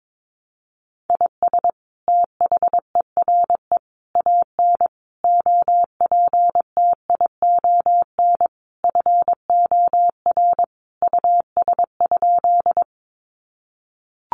Morse code, heard as a keyed tone, about 22 wpm, 700 Hz. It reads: IS THERE AN OPTION FOR US?
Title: Sentences from Top 1300 Words 22wpm